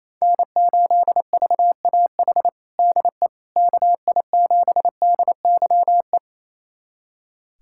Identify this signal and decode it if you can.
Morse code, heard as a keyed tone, about 28 wpm, 700 Hz. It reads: N8VAH DE KI7DYE
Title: USA Callsign DE Callsign 28wpm